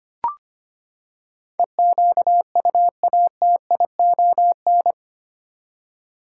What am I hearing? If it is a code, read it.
Morse code, heard as a keyed tone, about 25 wpm, 700 Hz. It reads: EQUATION
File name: Eight Letter Words - 25wpm